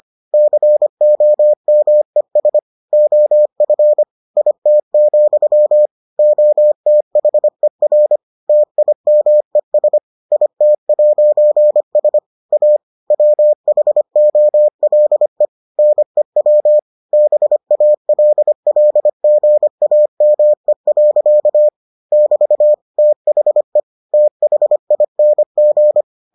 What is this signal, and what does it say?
Morse code, heard as a keyed tone, about 25 wpm, 600 Hz. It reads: COMES OF IT, OTHER TIMES IT'S A WHOLE NEW BALLGAME. <BT> THE THING